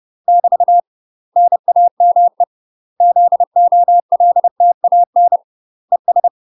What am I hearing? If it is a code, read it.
Morse code, heard as a keyed tone, about 30 wpm, 700 Hz. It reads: <BT> NAME ZOLTAN ES
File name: UK MASTER QSOs TRAINING / 30WPM / UK MASTER QSOs 0000